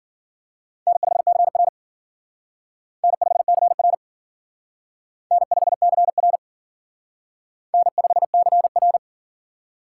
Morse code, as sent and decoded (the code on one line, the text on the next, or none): -. ..... -.-. .-. / -. ..... -.-. .-. / -. ..... -.-. .-. / -. ..... -.-. .-.
N5CR N5CR N5CR N5CR